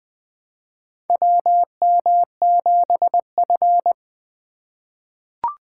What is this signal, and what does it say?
Morse code, heard as a keyed tone, about 20 wpm, 700 Hz. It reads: WM7F
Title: USA Call Signs - 4x Character Spacing - 20wpm